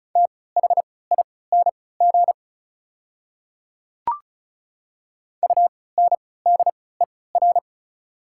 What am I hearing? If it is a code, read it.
Morse code, heard as a keyed tone, about 34 wpm, spaced as for 20 wpm, 700 Hz. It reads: THING UNDER
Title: 3x Character Spacing - Top 300 Words - 35wpm